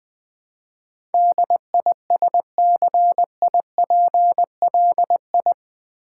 Morse code, heard as a keyed tone, about 20 wpm, 700 Hz. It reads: DISCIPLI
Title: Build A Word - 4 Letters At A Time - 10-Letter Words - 20wpm